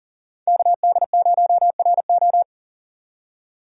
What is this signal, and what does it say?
Morse code, heard as a keyed tone, about 40 wpm, 700 Hz. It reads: KD0RO
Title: USA Call Signs - 5 Characters - 40wpm